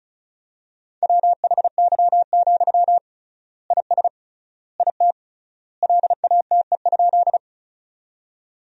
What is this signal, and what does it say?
Morse code, heard as a keyed tone, about 34 wpm, 700 Hz. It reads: WHY, IS IT LATE?
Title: 3x Word-Spacing - Sentences from Top 300 Words - 35wpm